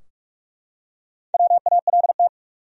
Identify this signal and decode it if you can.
Morse code, heard as a keyed tone, about 43 wpm, 700 Hz. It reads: WALT
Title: USA Common Names - Max 5 Letters - 45wpm